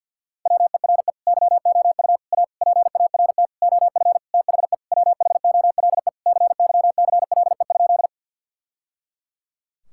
Morse code, heard as a keyed tone, about 50 wpm, 700 Hz. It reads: WERE YOU A PART OF THE WHOLE CYCLE?
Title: Sentences from Top 1000 Words - Review Part 6 - 50wpm